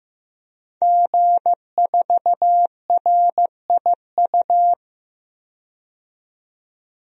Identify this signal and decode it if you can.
Morse code, heard as a keyed tone, about 15 wpm, 700 Hz. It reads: G4RIU
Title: DXCC-100 Call Signs - 5 Characters - 15wpm